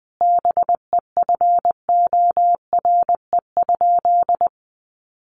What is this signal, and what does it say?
Morse code, heard as a keyed tone, about 20 wpm, 700 Hz. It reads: BEFORE?
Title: Sentences from Top 900 Words 20wpm